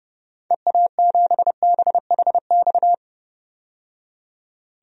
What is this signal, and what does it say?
Morse code, heard as a keyed tone, about 30 wpm, 700 Hz. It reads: EA7BHX
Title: DXCC-100 Call Signs - 6 Characters - 30wpm